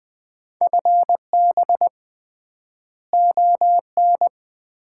Morse code, heard as a keyed tone, about 21 wpm, 700 Hz. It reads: FB ON
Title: Common QSO Phrases - 3x Word-Spacing - 20wpm